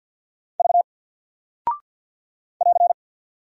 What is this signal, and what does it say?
Morse code, heard as a keyed tone, about 48 wpm, 700 Hz. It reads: V <AR>
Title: Single Character (Letter:Number:Punctuation:Prosign) 50wpm